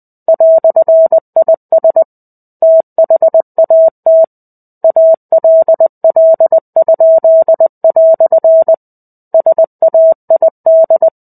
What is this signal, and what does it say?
Morse code, heard as a keyed tone, about 20 wpm, 650 Hz. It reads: "IS THAT ALL?" SAID
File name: 05-Chapter-05 20 WPM